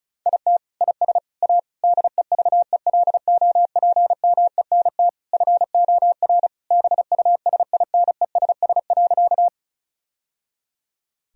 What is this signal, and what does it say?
Morse code, heard as a keyed tone, about 34 wpm, 700 Hz. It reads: IT IS A DEVELOPMENT FOR BUSINESS.
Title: Sentences from Top 1000 Words - Review Part 4 - 35wpm